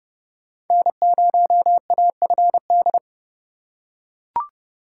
Morse code, heard as a keyed tone, about 28 wpm, 700 Hz. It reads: N0AFD